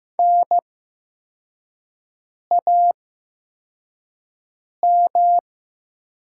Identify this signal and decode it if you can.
Morse code, heard as a keyed tone, about 15 wpm, spaced as for 4 wpm, 700 Hz. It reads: NAM
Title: 8x Character Spacing - Top 500 Words - 15wpm